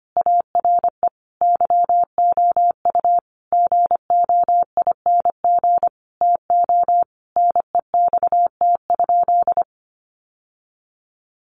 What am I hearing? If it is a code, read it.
Morse code, heard as a keyed tone, about 25 wpm, 700 Hz. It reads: ARE YOU GOING TO NEXT?
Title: Sentences from Top 1200 Words 25wpm